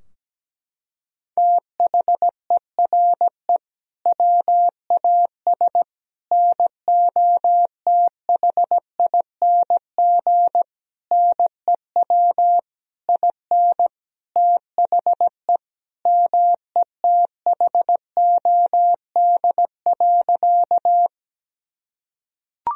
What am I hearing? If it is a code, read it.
Morse code, heard as a keyed tone, about 17 wpm, 700 Hz. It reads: THERE WAS NOTHING NEW IN THE METHOD.
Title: Sentences from Top 1000 Words - Review Part 3 - 17wpm